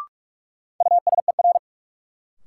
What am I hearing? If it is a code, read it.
Morse code, heard as a keyed tone, about 46 wpm, 700 Hz. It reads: USER
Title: Sentences from Top 1000 Words - Review Part 5 - New Word Drill - 45wpm